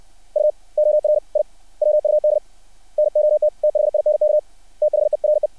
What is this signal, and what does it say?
Morse code, heard as a keyed tone, about 60 wpm, 600 Hz. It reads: U OR I WUD NOT THINK THERE
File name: archived / CRSnet